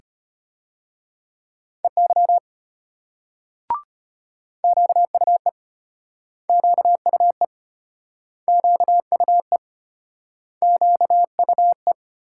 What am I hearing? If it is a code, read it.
Morse code, heard as a keyed tone, about 33 wpm, 700 Hz. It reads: EY QUE QUE QUE QUE